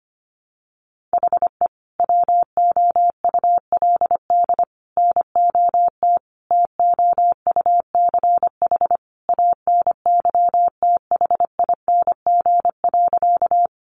Morse code, heard as a keyed tone, about 25 wpm, 700 Hz. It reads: HE WOULD NOT TOUCH ANYTHING.
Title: Sentences from Top 1200 Words 25wpm